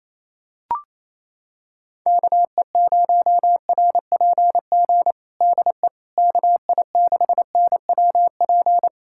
Morse code, heard as a keyed tone, about 28 wpm, 700 Hz. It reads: KE0RPG DE KI6NWP
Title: USA Callsign DE Callsign - 6 Characters - 28wpm